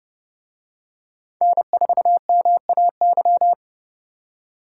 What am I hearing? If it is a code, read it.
Morse code, heard as a keyed tone, about 29 wpm, 700 Hz. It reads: N4MAY